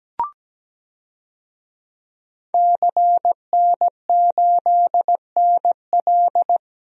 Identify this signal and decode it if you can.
Morse code, heard as a keyed tone, about 17 wpm, 700 Hz. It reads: CN8NL
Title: DXCC-100 Call Signs - 5 Characters - 17wpm